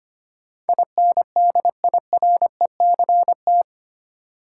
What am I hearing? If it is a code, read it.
Morse code, heard as a keyed tone, about 25 wpm, 700 Hz. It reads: INDIRECT
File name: Eight Letter Words - 25wpm